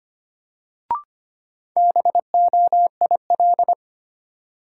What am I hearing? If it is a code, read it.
Morse code, heard as a keyed tone, about 25 wpm, 700 Hz. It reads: BOIL
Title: Sets of 5 Words Encore - New Word Drill - 25wpm